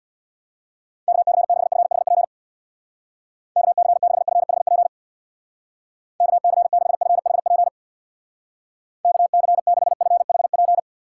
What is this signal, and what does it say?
Morse code, heard as a keyed tone, about 60 wpm, 700 Hz. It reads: KC6FHP KC6FHP KC6FHP KC6FHP
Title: Speed Racing - USA 2x3 Call Signs 50wpm